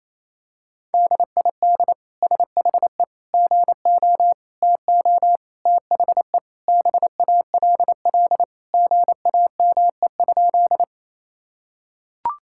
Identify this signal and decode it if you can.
Morse code, heard as a keyed tone, about 28 wpm, 700 Hz. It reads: DID SHE GO TO THE BALL GAME?
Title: Sentences from Top 1000 Words - Review Part 6 - 28wpm